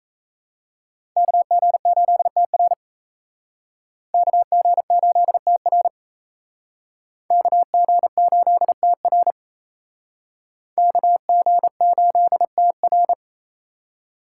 Morse code, heard as a keyed tone, about 35 wpm, 700 Hz. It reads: KG8TR KG8TR KG8TR KG8TR